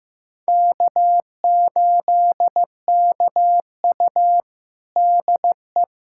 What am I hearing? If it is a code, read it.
Morse code, heard as a keyed tone, about 15 wpm, 700 Hz. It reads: K8KU DE
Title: USA Callsign DE Callsign - 4 Characters - 15wpm